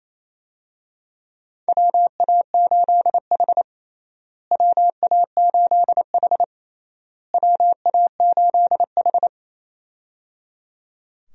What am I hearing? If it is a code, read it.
Morse code, heard as a keyed tone, about 28 wpm, 700 Hz. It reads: WA8H WA8H WA8H
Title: USA Call Signs - 4 Characters - Repeated 3x - 28wpm